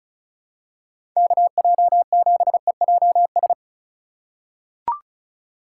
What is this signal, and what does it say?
Morse code, heard as a keyed tone, about 34 wpm, 700 Hz. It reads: KJ7EJS
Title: USA Call Signs - 6 Characters - 2x Character Spacing - 35wpm